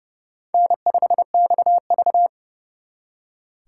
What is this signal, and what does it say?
Morse code, heard as a keyed tone, about 38 wpm, 700 Hz. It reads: N5XV